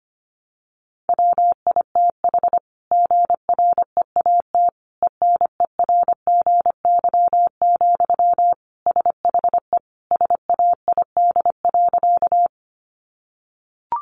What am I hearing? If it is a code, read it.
Morse code, heard as a keyed tone, about 25 wpm, 700 Hz. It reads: WITH GREAT ENERGY, SHE SAID.
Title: Sentences from Top 900 Words 25wpm